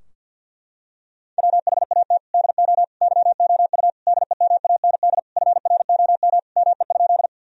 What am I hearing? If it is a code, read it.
Morse code, heard as a keyed tone, about 52 wpm, 700 Hz. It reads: WHAT DO YOU DEMAND FROM ME?
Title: Sentences from Top 1000 Words - Review Part 4 - 50wpm